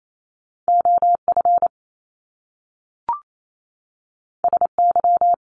Morse code, heard as a keyed tone, about 28 wpm, 700 Hz. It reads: OF SY
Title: Top Two Letter Combinations in Words 28wpm